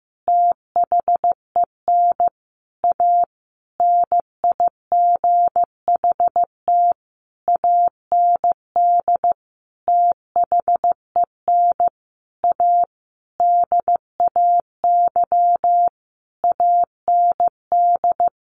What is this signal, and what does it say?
Morse code, heard as a keyed tone, about 15 wpm, 700 Hz. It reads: THEN A NIGHT AND THEN A DAY AND
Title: A Sound Of Thunder 15wpm